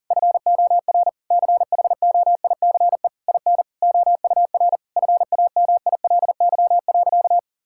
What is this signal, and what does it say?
Morse code, heard as a keyed tone, about 37 wpm, 700 Hz. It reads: FOR CHOICE IN OUR FAMILY.
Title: Sentences from Top 1000 Words - Review Part 6 - 40wpm